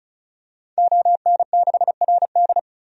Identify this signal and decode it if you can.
Morse code, heard as a keyed tone, about 36 wpm, 700 Hz. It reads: ON6RD
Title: DXCC-100 Call Signs - 5 Characters - 35wpm